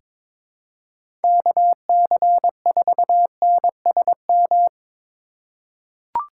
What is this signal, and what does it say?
Morse code, heard as a keyed tone, about 21 wpm, 700 Hz. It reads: KC4NSM
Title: USA Call Signs - 6 Characters - 2x Character Spacing - 22wpm